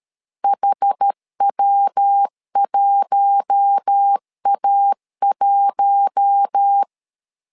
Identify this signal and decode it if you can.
Morse code, heard as a keyed tone, about 13 wpm, 800 Hz. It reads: HW1A1
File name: LEZIONI / 4 Q1 12wpm 2028